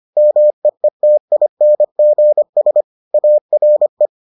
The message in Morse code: MEETINGS ARE

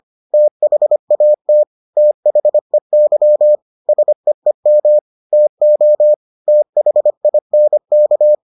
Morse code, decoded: THAT THEY SEEM TO THINK

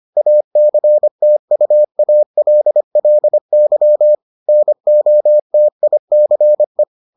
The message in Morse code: ACTUALLY NOTICE

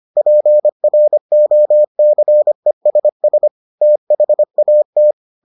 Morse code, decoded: PROCESS THAT

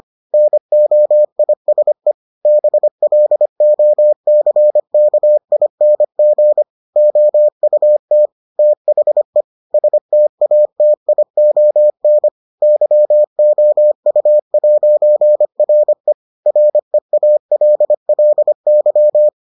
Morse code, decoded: NOISE BLOCKING OUT THE STATION YOU'RE REALLY

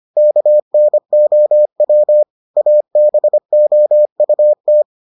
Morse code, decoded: KNOW ABOUT

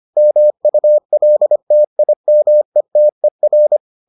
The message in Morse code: MULTIMETER

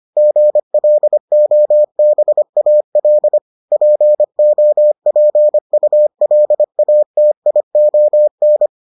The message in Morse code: GLOBAL POPULATION

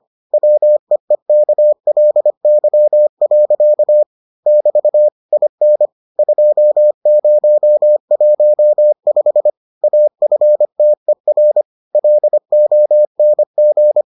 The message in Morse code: WEEKLY. <BT> IN 2015 AFTER LONG

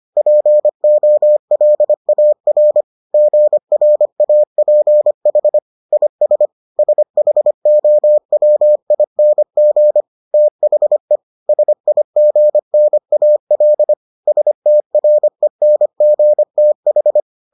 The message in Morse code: POLAR GRAPH IS SHOWING THE SIGNAL STRENGTH

